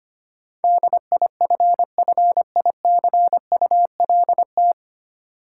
Morse code, heard as a keyed tone, about 25 wpm, 700 Hz. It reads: DIFFICULT
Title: Sets of 4 Words - New Word Drill - 25wpm